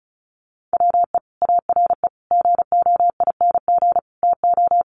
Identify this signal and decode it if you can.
Morse code, heard as a keyed tone, about 34 wpm, 700 Hz. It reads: WE ARE GOING TO
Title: Sets of 4 Words 35wpm